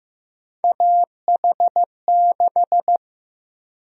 Morse code, heard as a keyed tone, about 15 wpm, 700 Hz. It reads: AH6
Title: USA Call Sign Prefixes - 15wpm